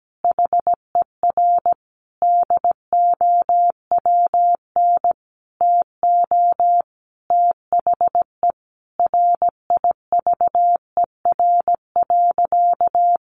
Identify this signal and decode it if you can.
Morse code, heard as a keyed tone, about 17 wpm, 700 Hz. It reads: HER DOWN TO THE RIVER.
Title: Sentences from Top 1500 Words 17wpm